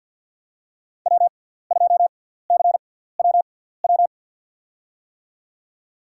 Morse code, heard as a keyed tone, about 52 wpm, spaced as for 17 wpm, 700 Hz. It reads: W2CWW
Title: USA Call Signs - 5 Characters - 6x Character Spacing - 50wpm